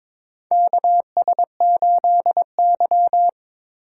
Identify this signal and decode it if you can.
Morse code, heard as a keyed tone, about 22 wpm, 700 Hz. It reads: KS8Y
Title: USA Call Signs - 4 Characters - 4x Character Spacing - 22wpm